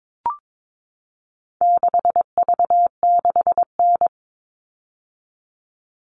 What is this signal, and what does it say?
Morse code, heard as a keyed tone, about 22 wpm, 700 Hz. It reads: 6V6N
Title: All Cut Numbers - 4-Digit 22wpm